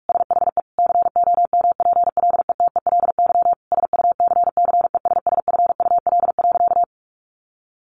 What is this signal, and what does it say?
Morse code, heard as a keyed tone, about 44 wpm, 700 Hz. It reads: SHE COMPLETELY SUCCESSFUL.
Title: Sentences from Top 1100 Words 45wpm